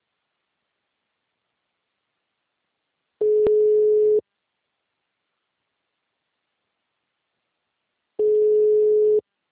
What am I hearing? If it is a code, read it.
Ringback tone (Europe, 425 Hz).
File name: out- -2027- - - .149